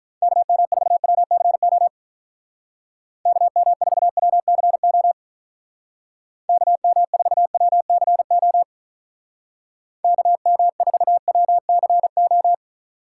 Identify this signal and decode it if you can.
Morse code, heard as a keyed tone, about 44 wpm, 700 Hz. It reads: KM4WCO KM4WCO KM4WCO KM4WCO